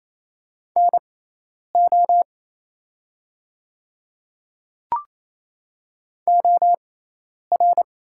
Morse code, heard as a keyed tone, about 28 wpm, spaced as for 10 wpm, 700 Hz. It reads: NO OR
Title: 6x Character Spacing - Top 200 Words - 28wpm